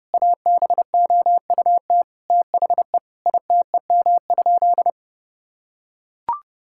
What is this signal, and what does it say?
Morse code, heard as a keyed tone, about 32 wpm, 700 Hz. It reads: ABOUT THE ITEM?
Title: Sentences from Top 1000 Words - Review Part 6 - 30wpm